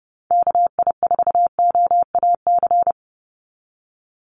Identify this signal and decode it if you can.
Morse code, heard as a keyed tone, about 30 wpm, 700 Hz. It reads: KI4OAC
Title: USA Call Signs 30wpm